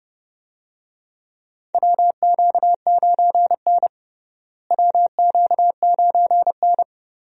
Morse code, heard as a keyed tone, about 31 wpm, 700 Hz. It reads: WQ9N WQ9N
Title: USA Call Signs - 4 Characters - Repeated 2x - 30wpm